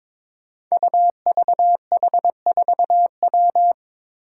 Morse code, heard as a keyed tone, about 23 wpm, 700 Hz. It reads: UVH4W
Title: Five Challenging Letter-Number Combinations 22wpm